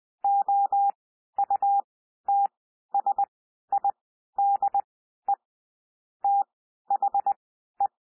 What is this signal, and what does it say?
Morse code, heard as a keyed tone, about 20 wpm, spaced as for 12 wpm, 800 Hz. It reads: OUTSIDE THE
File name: cwmp3 / P1 SOUND PASSAGE 20 wpm